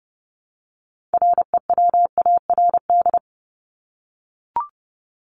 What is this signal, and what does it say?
Morse code, heard as a keyed tone, about 30 wpm, 700 Hz. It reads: REWARD
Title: Top 1751-2000 Words 30wpm